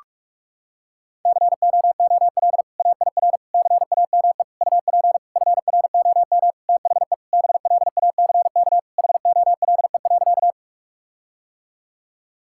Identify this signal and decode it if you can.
Morse code, heard as a keyed tone, about 46 wpm, 700 Hz. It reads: COOL AIR CAME UP FROM THE BLACK HOLE.
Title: Sentences from Top 1600 Words 45wpm